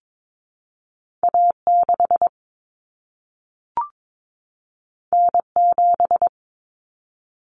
Morse code, heard as a keyed tone, about 22 wpm, 700 Hz. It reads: A6 N7
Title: Common Cut Numbers - 2-Digit 22wpm